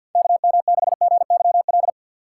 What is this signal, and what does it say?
Morse code, heard as a keyed tone, about 53 wpm, 700 Hz. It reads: KM6GYL
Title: USA Call Signs - 4x Character Spacing - 50wpm